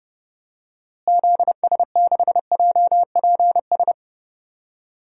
Morse code, heard as a keyed tone, about 32 wpm, 700 Hz. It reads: ZS6JPS
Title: DXCC-100 Call Signs - 6 Characters - Repeated 3x - 30wpm